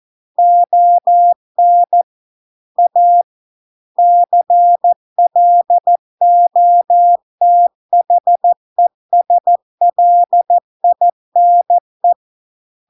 Morse code, heard as a keyed tone, about 14 wpm, 700 Hz. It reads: ON A CLOTHESLINE